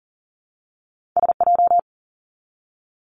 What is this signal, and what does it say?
Morse code, heard as a keyed tone, about 40 wpm, 700 Hz. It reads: S J
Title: Two Challenging Letter-Number Combinations 40wpm